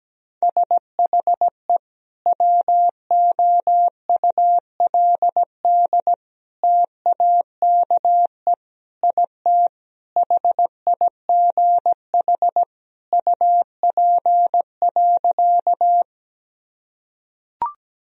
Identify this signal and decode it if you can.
Morse code, heard as a keyed tone, about 17 wpm, 700 Hz. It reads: SHE WOULD TAKE IT HIGH UP.
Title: Sentences from Top 1000 Words - Review Part 1 - 17wpm